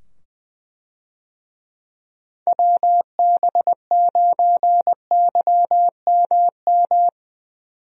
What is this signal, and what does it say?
Morse code, heard as a keyed tone, about 20 wpm, 700 Hz. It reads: WB9YMM
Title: USA Call Signs - 6 Characters - 20wpm